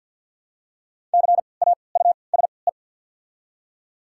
Morse code, heard as a keyed tone, about 53 wpm, spaced as for 29 wpm, 690 Hz. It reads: CAUSE
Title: 3x Character Spacing - Top 300 Words - 50wpm